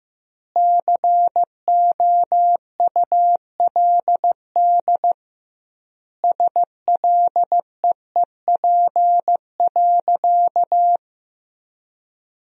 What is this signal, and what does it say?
Morse code, heard as a keyed tone, about 16 wpm, 700 Hz. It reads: COULD SLEEP.
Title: 2x Word-Spacing - Sentences from Top 500 Words - 15wpm